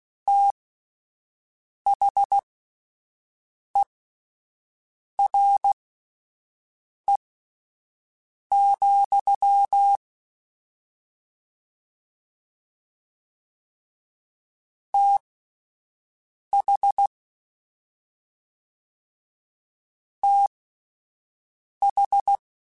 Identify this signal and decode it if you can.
Morse code, heard as a keyed tone, about 16 wpm, spaced as for 5 wpm, 800 Hz. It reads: THERE, TH TH